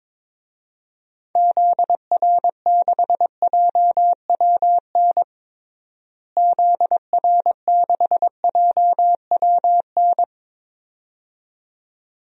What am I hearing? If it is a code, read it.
Morse code, heard as a keyed tone, about 22 wpm, 700 Hz. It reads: ZR6JWN ZR6JWN
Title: DXCC-100 Call Signs - 6 Characters - Repeated 2x - 22wpm